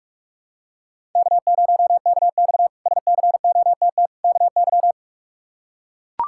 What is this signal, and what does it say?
Morse code, heard as a keyed tone, about 41 wpm, 700 Hz. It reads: K0KX SCOTT KY
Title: K1USN SST QSO Practice - 45wpm